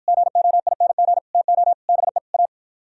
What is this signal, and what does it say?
Morse code, heard as a keyed tone, about 56 wpm, 700 Hz. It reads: GOING TO BE A